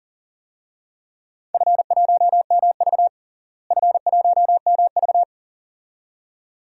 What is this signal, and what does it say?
Morse code, heard as a keyed tone, about 40 wpm, 700 Hz. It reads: F1MV F1MV
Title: DXCC-100 Call Signs - 4 Characters - Repeated 2x - 40wpm